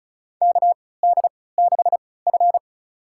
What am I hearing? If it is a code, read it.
Morse code, heard as a keyed tone, about 36 wpm, spaced as for 20 wpm, 700 Hz. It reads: KD6F
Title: USA Call Signs - 3x Character Spacing - 35wpm